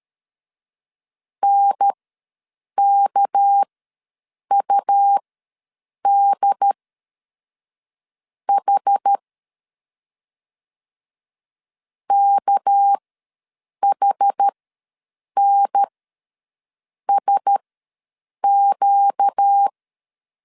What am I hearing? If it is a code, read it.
Morse code, heard as a keyed tone, about 13 wpm, spaced as for 6 wpm, 800 Hz. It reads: NKUDH KHNSQ
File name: LEZIONI / 6 PKX0 / 6 PK ws 12wpm 3116